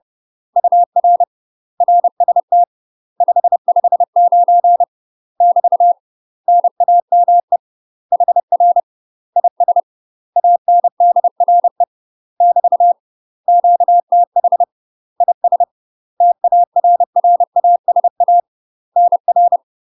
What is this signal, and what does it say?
Morse code, heard as a keyed tone, about 30 wpm, 700 Hz. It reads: UR RST 559 <BT> NAME HR IS ANDRE <BT> QTH IS TARRASA NR
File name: UK MASTER QSOs TRAINING / 30WPM / UK MASTER QSOs 0049